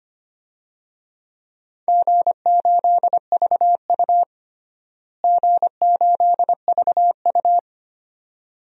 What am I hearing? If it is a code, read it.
Morse code, heard as a keyed tone, about 25 wpm, 700 Hz. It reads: G8VU G8VU
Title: DXCC-100 Call Signs - 4 Characters - Repeated 2x - 25wpm